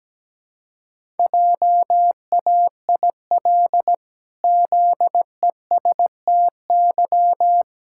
Morse code, heard as a keyed tone, about 17 wpm, 700 Hz. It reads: JAIL ZESTY
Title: Pangram Challenge v2 - 17wpm